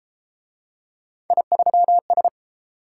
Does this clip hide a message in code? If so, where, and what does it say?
Yes: Morse code from 1.3 to 2.3 s, heard as a keyed tone, about 30 wpm, 700 Hz. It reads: I3S